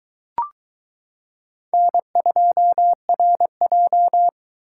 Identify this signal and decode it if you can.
Morse code, heard as a keyed tone, about 22 wpm, 700 Hz. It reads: N2RJ